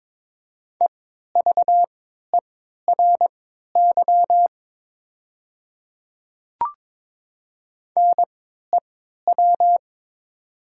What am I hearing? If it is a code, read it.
Morse code, heard as a keyed tone, about 22 wpm, spaced as for 12 wpm, 700 Hz. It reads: EVERY NEW
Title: 3x Character Spacing - Top 200 Words - 22wpm